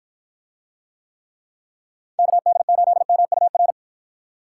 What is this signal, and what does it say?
Morse code, heard as a keyed tone, about 53 wpm, 700 Hz. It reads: KD8MUR